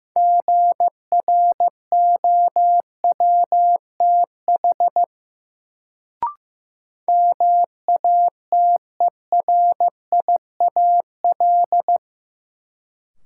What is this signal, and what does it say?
Morse code, heard as a keyed tone, about 15 wpm, 700 Hz. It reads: GROWTH MATERIAL
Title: Sentences from Top 1000 Words - Review Part 5 - New Word Drill - 15wpm